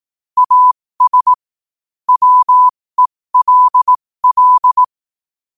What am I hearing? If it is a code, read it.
Morse code, heard as a keyed tone, about 18 wpm, spaced as for 15 wpm, 1000 Hz. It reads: AS WELL
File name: qotdfiles / 15wpm